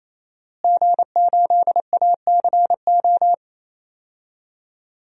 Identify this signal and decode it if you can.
Morse code, heard as a keyed tone, about 27 wpm, 700 Hz. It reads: G8ACO
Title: DXCC-100 Call Signs - 5 Characters - Repeated 3x - 28wpm